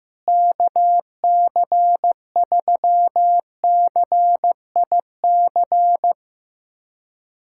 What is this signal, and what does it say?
Morse code, heard as a keyed tone, about 15 wpm, 700 Hz. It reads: KC3CIC
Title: USA Callsign DE Callsign - 6 Characters - 15wpm